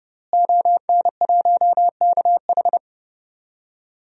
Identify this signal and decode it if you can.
Morse code, heard as a keyed tone, about 32 wpm, 700 Hz. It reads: ON1KH
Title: DXCC-100 Call Signs - 5 Characters - Repeated 3x - 30wpm